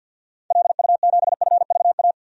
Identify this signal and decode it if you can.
Morse code, heard as a keyed tone, about 51 wpm, 700 Hz. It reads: LU7FVA
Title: DXCC-100 Call Signs - 6 Characters - 50wpm